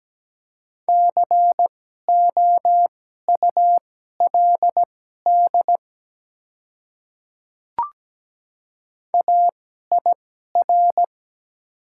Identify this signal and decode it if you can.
Morse code, heard as a keyed tone, about 17 wpm, spaced as for 12 wpm, 700 Hz. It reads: COULD AIR
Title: 2x Character Spacing - Top 300 Words - 17wpm